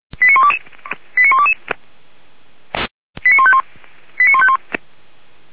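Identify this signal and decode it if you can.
Five-tone selective call (ZVEI), heard as quick runs of five tones, tones of about 80 ms. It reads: 89122 89151 89151
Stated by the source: ZVEI five-tone selective call 89200 received by SDR.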